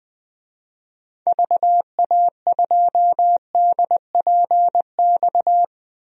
Morse code, heard as a keyed tone, about 20 wpm, 700 Hz. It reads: VA2DPX
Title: DXCC-100 Call Signs - 6 Characters - 20wpm